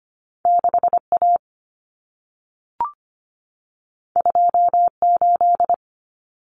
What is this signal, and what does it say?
Morse code, heard as a keyed tone, about 25 wpm, 700 Hz. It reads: 6A 28
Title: Common Cut Numbers - 2-Digit 25wpm